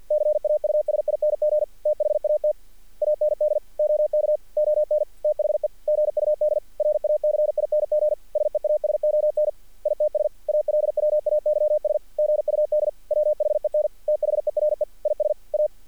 Morse code, heard as a keyed tone, about 48 wpm, 600 Hz. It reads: CAUSING THAT AND OK ON THE GUD RACING SEASON ITS ALWAYS GUD WHEN THERE IS A
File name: archived / CRSnet